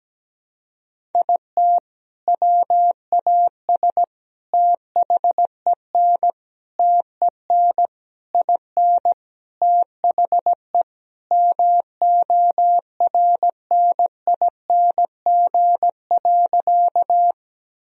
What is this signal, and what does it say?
Morse code, heard as a keyed tone, about 17 wpm, 700 Hz. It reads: IT WAS THEN TEN IN THE MORNING.
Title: Sentences from Top 1000 Words - Review Part 2 - 17wpm